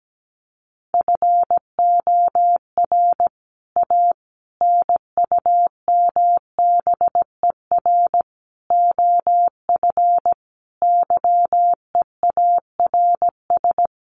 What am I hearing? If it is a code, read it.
Morse code, heard as a keyed tone, about 17 wpm, 700 Hz. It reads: FOR A NUMBER OF YEARS
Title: Sets of 5 Words Encore 17wpm